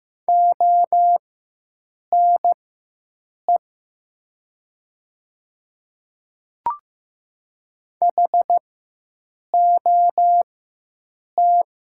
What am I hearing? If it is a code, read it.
Morse code, heard as a keyed tone, about 15 wpm, spaced as for 7 wpm, 700 Hz. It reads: ONE HOT
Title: 4x Character Spacing - Top 300 Words - 15wpm